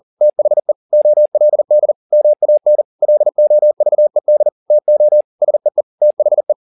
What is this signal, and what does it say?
Morse code, heard as a keyed tone, about 40 wpm, 600 Hz. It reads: THE OLD MAN LOVED TO SEE THE